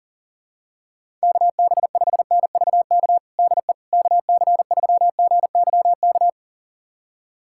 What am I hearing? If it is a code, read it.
Morse code, heard as a keyed tone, about 40 wpm, 700 Hz. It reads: KB5NVK DE KC3GYK
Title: USA Callsign DE Callsign 40wpm